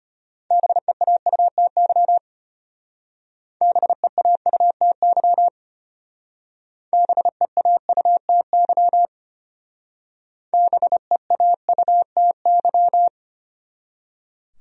Morse code, heard as a keyed tone, about 32 wpm, 700 Hz. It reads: BEAUTY BEAUTY BEAUTY BEAUTY